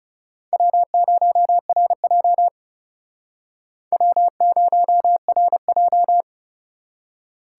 Morse code, heard as a keyed tone, about 40 wpm, 700 Hz. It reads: W0RJ W0RJ